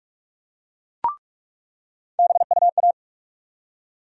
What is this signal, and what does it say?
Morse code, heard as a keyed tone, about 46 wpm, 700 Hz. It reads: BUA
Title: USA Call Sign Postfixes - 3 Letters 45wpm